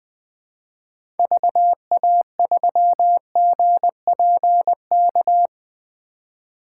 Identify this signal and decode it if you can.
Morse code, heard as a keyed tone, about 20 wpm, 700 Hz. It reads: VA3GPK
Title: 2023 POTA Activator Call Signs - 20wpm